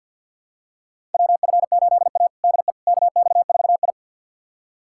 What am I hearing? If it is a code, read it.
Morse code, heard as a keyed tone, about 46 wpm, 700 Hz. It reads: WF8A DE KX4I
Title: USA Callsign DE Callsign - 4 Characters - 50wpm